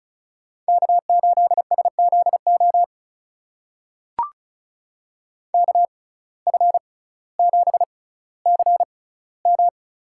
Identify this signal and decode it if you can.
Morse code, heard as a keyed tone, about 36 wpm, spaced as for 18 wpm, 700 Hz. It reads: K8SZO KF7CM
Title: USA Call Signs - 5 Characters - 6x Character Spacing - 35wpm